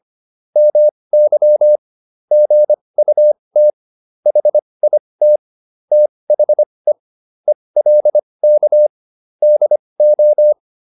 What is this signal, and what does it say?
Morse code, heard as a keyed tone, about 25 wpm, spaced as for 20 wpm, 600 Hz. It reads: MY GUT HIT THE ELK DO